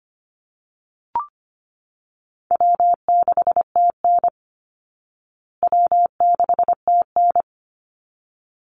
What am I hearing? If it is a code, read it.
Morse code, heard as a keyed tone, about 25 wpm, 700 Hz. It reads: W6TN W6TN
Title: Easy Call Signs Repeated 2x - 25wpm